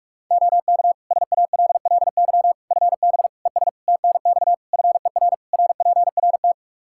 Morse code, heard as a keyed tone, about 41 wpm, 700 Hz. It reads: OK SALLY FB ES TNX FER RPRT
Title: Practice QSO - Exchange 2 - Info - 45wpm